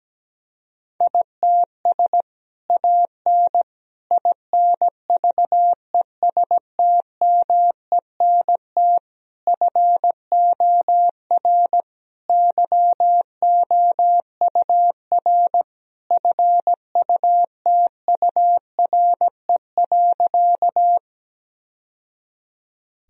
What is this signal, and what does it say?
Morse code, heard as a keyed tone, about 17 wpm, 700 Hz. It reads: ITS AN INVESTMENT FOR YOUR FUTURE.
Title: Sentences from Top 1000 Words - Review Part 4 - 17wpm